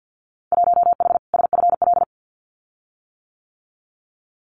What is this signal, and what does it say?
Morse code, heard as a keyed tone, about 50 wpm, 700 Hz. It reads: 1H SFL
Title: Winter Field Day Exchanges 50wpm